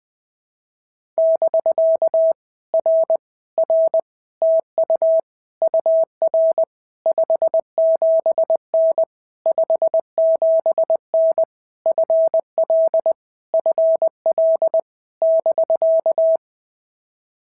Morse code, heard as a keyed tone, about 20 wpm, 650 Hz. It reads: <BK> R R TU UR 57N 57N FL FL <BK>
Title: POTA QSO Practice - 20wpm